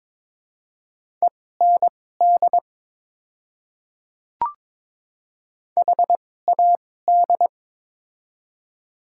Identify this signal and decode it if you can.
Morse code, heard as a keyed tone, about 22 wpm, spaced as for 16 wpm, 700 Hz. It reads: END HAD
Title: 2x Character Spacing - Top 200 Words - 22wpm